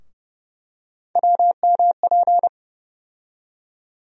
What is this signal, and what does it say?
Morse code, heard as a keyed tone, about 30 wpm, 700 Hz. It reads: WMP
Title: USA Call Sign Postfixes - 3 Letters 30wpm